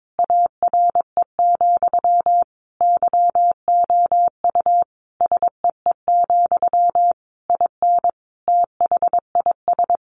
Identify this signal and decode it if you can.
Morse code, heard as a keyed tone, about 22 wpm, 700 Hz. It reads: ARE, YOU SEE, IN THIS
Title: Sentences from Top 1200 Words 22wpm